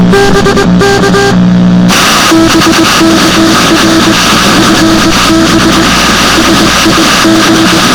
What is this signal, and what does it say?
Morse code, heard as a keyed tone, about 20 wpm, 320 Hz, with noise. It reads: BK R FB SIG